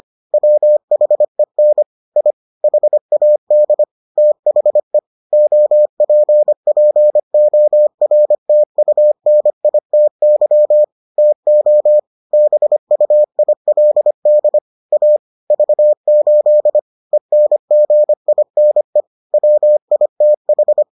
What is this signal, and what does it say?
Morse code, heard as a keyed tone, about 25 wpm, 600 Hz. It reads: WHEN I HAD THE OPPORTUNITY TO BUILD A V8 ENGINE WITH